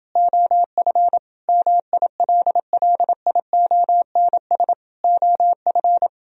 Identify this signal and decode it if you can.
Morse code, heard as a keyed tone, about 28 wpm, 700 Hz. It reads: OF MILLIONS OF